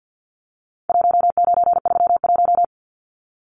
Morse code, heard as a keyed tone, about 50 wpm, 700 Hz. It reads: 1931
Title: Numbers - 4-Digit 50wpm